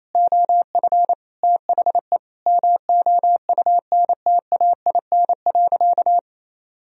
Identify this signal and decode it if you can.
Morse code, heard as a keyed tone, about 28 wpm, 700 Hz. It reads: OF THE MOUNTAIN.
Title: Sentences from Top 1000 Words - Review Part 1 - 28wpm